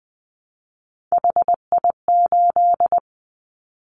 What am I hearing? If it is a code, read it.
Morse code, heard as a keyed tone, about 20 wpm, 700 Hz. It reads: HI8
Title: Callsign Prefixes for Top 100 DXCC - 20wpm